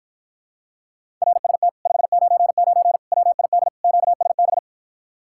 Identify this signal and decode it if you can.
Morse code, heard as a keyed tone, about 53 wpm, 700 Hz. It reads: RST 599 WID QSB